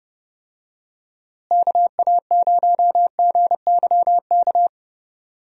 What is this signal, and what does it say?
Morse code, heard as a keyed tone, about 31 wpm, 700 Hz. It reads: KA0GYK
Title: USA Call Signs - 6 Characters - Repeated 2x - 30wpm